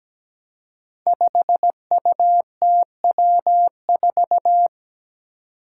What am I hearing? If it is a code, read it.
Morse code, heard as a keyed tone, about 17 wpm, 700 Hz. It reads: 5UTW4
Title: Five Challenging Letter-Number Combinations 17wpm